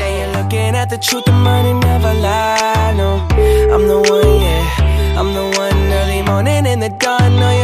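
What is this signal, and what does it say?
Ringback tone (Europe, 425 Hz), with noise.
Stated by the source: O2 Czech Republic ringback (welcome melody) tone heard by callers before the call is answered.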